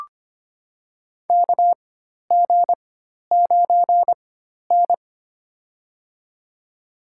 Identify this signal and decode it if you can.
Morse code, heard as a keyed tone, about 25 wpm, spaced as for 12 wpm, 700 Hz. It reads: KG9N
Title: USA Call Signs - 4 Characters - 4x Character Spacing - 25wpm